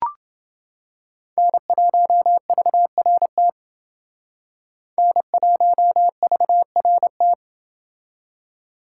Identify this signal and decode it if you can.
Morse code, heard as a keyed tone, about 29 wpm, 700 Hz. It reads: N1VRT N1VRT